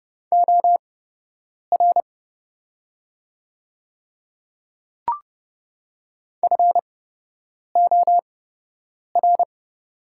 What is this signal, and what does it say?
Morse code, heard as a keyed tone, about 30 wpm, spaced as for 8 wpm, 700 Hz. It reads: OR FOR
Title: 8x Character Spacing - Top 100 Words - 30wpm